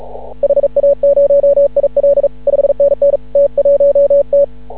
Morse code, heard as a keyed tone, about 36 wpm, 560 Hz. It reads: HA0IL 5NN T1T
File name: CW / ha6oi 2